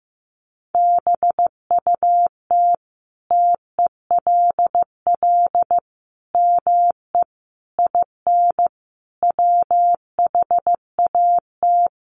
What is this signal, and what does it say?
Morse code, heard as a keyed tone, about 14 wpm, 700 Hz. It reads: BUT TELL ME IN WHAT
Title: Sentences from Top 1300 Words 15wpm